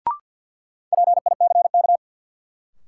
Morse code, heard as a keyed tone, about 48 wpm, 700 Hz. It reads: PICK
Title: Sentences from Top 1000 Words - New Word Drill - 50wpm